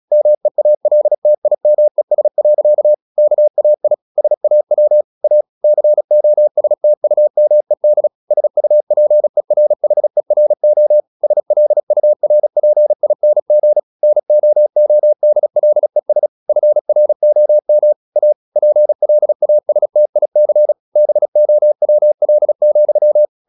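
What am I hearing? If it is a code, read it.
Morse code, heard as a keyed tone, about 36 wpm, 600 Hz. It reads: MEALTIMES. KAI SAW A COSTUMED SUPERHERO SLURPING NOODLES FROM A PLASTIC BOWL,
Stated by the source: Morse practice at 36 wpm, text of Ken Follett's novel Never, from CW med Gnister.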